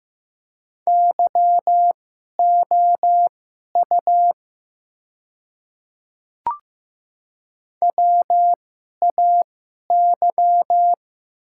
Morse code, heard as a keyed tone, about 15 wpm, spaced as for 11 wpm, 700 Hz. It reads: YOU WAY
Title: 2x Character Spacing - Top 100 Words - 15wpm